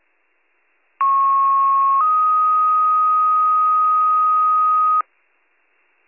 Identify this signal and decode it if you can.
Two-tone page, heard as a one-second tone, then a three-second tone, about 1090 then 1250 Hz. It is heard from 1.0 to 5.0 s.